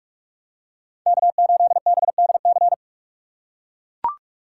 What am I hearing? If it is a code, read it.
Morse code, heard as a keyed tone, about 42 wpm, 700 Hz. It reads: K8BDC
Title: USA Call Signs - 5 Characters - 3x Character Spacing - 45wpm